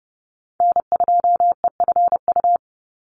Morse code, heard as a keyed tone, about 30 wpm, 700 Hz. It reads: N2EFU
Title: USA Call Signs 30wpm